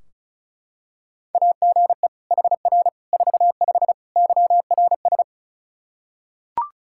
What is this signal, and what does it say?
Morse code, heard as a keyed tone, about 36 wpm, 700 Hz. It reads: AGE HR 45 YRS
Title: Practice QSO - Exchange 3 - More Info - 35wpm